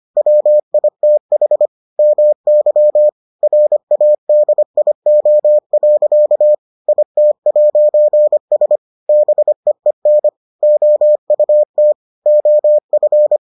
Morse code, heard as a keyed tone, about 25 wpm, 600 Hz. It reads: WITH MY RADIO. IT'S BEEN OUT OF